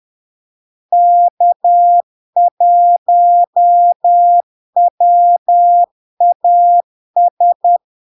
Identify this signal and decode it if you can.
Morse code, heard as a keyed tone, about 10 wpm, 700 Hz. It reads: K1WAS
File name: LARK-FD-2022-Log-in-CW-10wpm-700-Hz